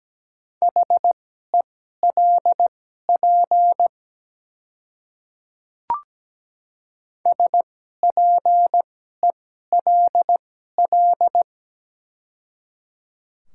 Morse code, heard as a keyed tone, about 17 wpm, spaced as for 12 wpm, 700 Hz. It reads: HELP SPELL
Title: 2x Character Spacing - Top 200 Words - 17wpm